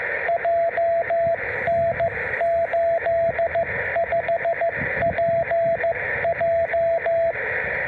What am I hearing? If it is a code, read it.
Morse code, heard as a keyed tone, about 14 wpm, 690 Hz, with radio noise. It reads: JN85PJ